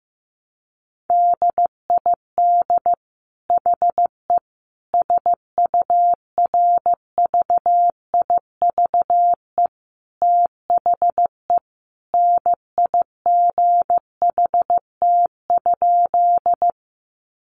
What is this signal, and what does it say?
Morse code, heard as a keyed tone, about 15 wpm, 700 Hz. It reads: DID HE SURVIVE THE NIGHT?
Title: Sentences from Top 1300 Words 15wpm